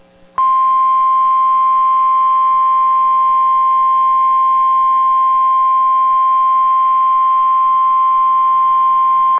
Emergency Alert System (SAME) tones (the 1050 Hz attention tone). It is heard from 1.0 to 9.1 s.